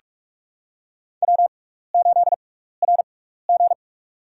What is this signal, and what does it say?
Morse code, heard as a keyed tone, about 46 wpm, spaced as for 15 wpm, 700 Hz. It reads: W8RG
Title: USA Call Signs - 4 Characters - 6x Character Spacing - 45wpm